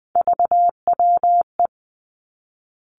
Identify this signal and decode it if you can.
Morse code, heard as a keyed tone, about 20 wpm, 700 Hz. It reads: VWE
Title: Three Challenging Letter-Number Combinations 20wpm